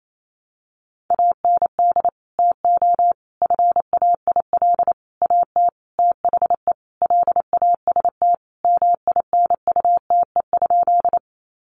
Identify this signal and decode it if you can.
Morse code, heard as a keyed tone, about 28 wpm, 700 Hz. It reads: AND TO FAIL AT THE LAST MINUTE?
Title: Sentences from Top 1100 Words 28wpm